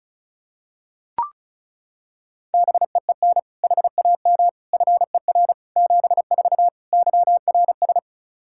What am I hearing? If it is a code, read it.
Morse code, heard as a keyed tone, about 36 wpm, 700 Hz. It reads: BEEN HAM FER 74 YRS
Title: K1USN SST QSO Practice 35wpm